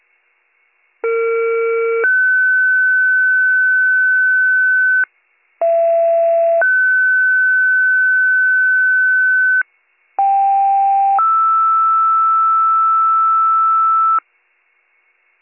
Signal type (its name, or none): two-tone page